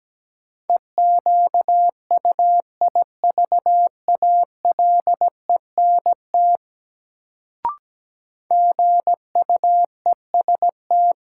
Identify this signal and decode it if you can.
Morse code, heard as a keyed tone, about 17 wpm, 700 Hz. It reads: EQUIVALENT GUEST
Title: Sentences from Top 1600 Words - New Word Drill Alt - 17wpm